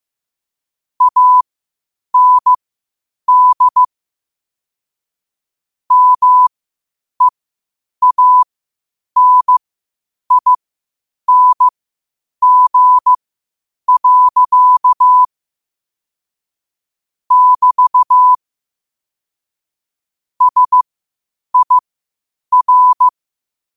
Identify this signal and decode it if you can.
Morse code, heard as a keyed tone, about 15 wpm, spaced as for 8 wpm, 1000 Hz. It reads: AND MEANING. <BT> SIR
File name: qotdfiles / 8wpm